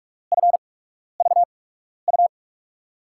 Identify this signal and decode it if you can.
Morse code, heard as a keyed tone, about 46 wpm, spaced as for 12 wpm, 700 Hz. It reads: FVU
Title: USA Call Signs - 6 Characters - 8x Character Spacing - 45wpm